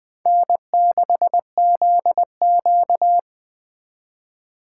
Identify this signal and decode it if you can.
Morse code, heard as a keyed tone, about 24 wpm, 700 Hz. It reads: N6ZQ
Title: Speed Racing - USA 1x2 Call Signs 20wpm